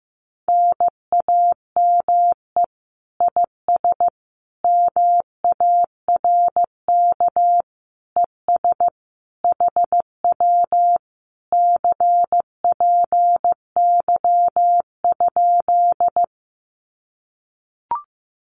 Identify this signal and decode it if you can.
Morse code, heard as a keyed tone, about 15 wpm, 700 Hz. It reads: NAME IS MARK ES HW CPY?
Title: Common QSO Phrases Encore 15wpm